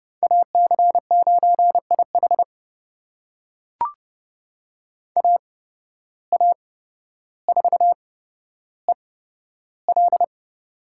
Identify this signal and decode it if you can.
Morse code, heard as a keyed tone, about 29 wpm, spaced as for 13 wpm, 700 Hz. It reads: AC9IH AA4EL
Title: USA Call Signs - 5 Characters - 8x Character Spacing - 30wpm